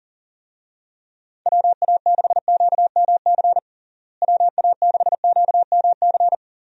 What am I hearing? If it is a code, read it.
Morse code, heard as a keyed tone, about 37 wpm, 700 Hz. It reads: WA6QMC WA6QMC
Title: USA Call Signs - 6 Characters - Repeated 2x - 40wpm